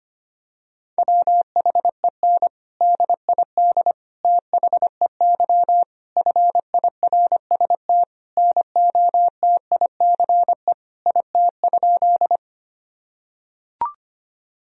Morse code, heard as a keyed tone, about 25 wpm, 700 Hz. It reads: WHEN DID THEY FIRST NOTICE IT?
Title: Sentences from Top 1000 Words - Review Part 2 - 25wpm